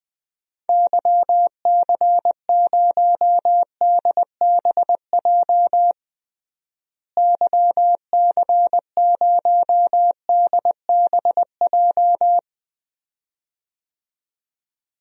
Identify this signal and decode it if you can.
Morse code, heard as a keyed tone, about 20 wpm, 700 Hz. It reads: YC0DBJ YC0DBJ
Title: DXCC-100 Call Signs - 6 Characters - Repeated 2x - 20wpm